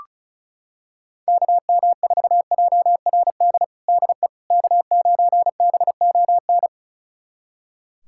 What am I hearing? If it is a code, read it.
Morse code, heard as a keyed tone, about 36 wpm, 700 Hz. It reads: KM4JRD DE K9BON
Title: USA Callsign DE Callsign 35wpm